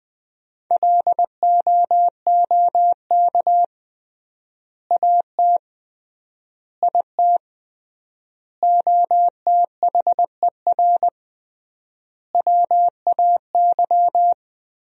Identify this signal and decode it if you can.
Morse code, heard as a keyed tone, about 20 wpm, 700 Hz. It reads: LOOK AT IT OTHER WAY
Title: 3x Word-Spacing - Sentences from Top 200 Words - 20wpm